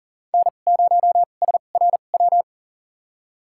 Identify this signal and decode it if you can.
Morse code, heard as a keyed tone, about 37 wpm, spaced as for 28 wpm, 700 Hz. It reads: N0SRW
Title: USA Call Signs - 5 Characters - 2x Character Spacing - 40wpm